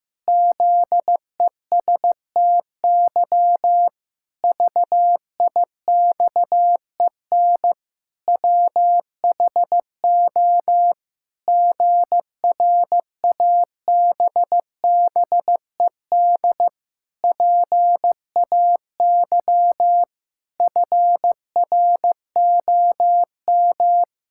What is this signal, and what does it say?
Morse code, heard as a keyed tone, about 15 wpm, 700 Hz. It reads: ZESTY VIXEN WHO GRABBED PAY FROM
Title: Pangram Challenge v2 - 15wpm